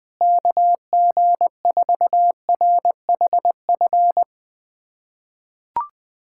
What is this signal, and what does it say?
Morse code, heard as a keyed tone, about 21 wpm, 700 Hz. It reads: KG4RHF
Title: USA Callsign DE Callsign - 6 Characters - 20wpm